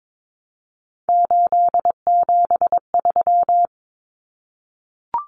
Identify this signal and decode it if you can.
Morse code, heard as a keyed tone, about 22 wpm, 700 Hz. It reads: 873
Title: Numbers - 3-Digit 22wpm